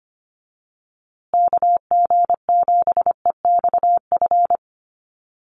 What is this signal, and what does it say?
Morse code, heard as a keyed tone, about 25 wpm, 700 Hz. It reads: KG7EXF
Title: USA Call Signs Repeated 3x 25wpm